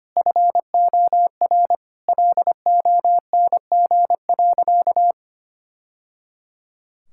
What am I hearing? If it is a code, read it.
Morse code, heard as a keyed tone, about 25 wpm, 700 Hz. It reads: FOR LONG.
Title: Sentences from Top 1000 Words - Review Part 5 - 25wpm